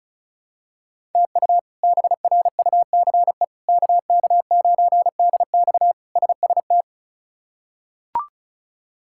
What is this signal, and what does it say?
Morse code, heard as a keyed tone, about 36 wpm, 700 Hz. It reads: TU BRUCE KK9DX SST
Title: K1USN SST QSO Practice - 35wpm